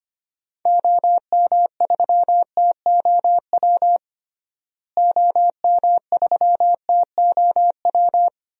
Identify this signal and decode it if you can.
Morse code, heard as a keyed tone, about 25 wpm, 700 Hz. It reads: OM3TOW OM3TOW
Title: DXCC-100 Call Signs - 6 Characters - Repeated 2x - 25wpm